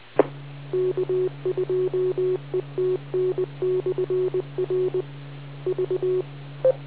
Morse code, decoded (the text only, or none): E K2ETN/R V